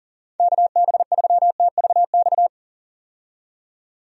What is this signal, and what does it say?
Morse code, heard as a keyed tone, about 40 wpm, 700 Hz. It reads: KB3TVX
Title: USA Call Signs - 6 Characters - 40wpm